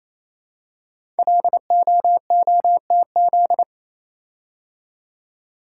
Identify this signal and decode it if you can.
Morse code, heard as a keyed tone, about 27 wpm, 700 Hz. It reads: LOOTZ
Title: Code Groups - 5 Letters - 28wpm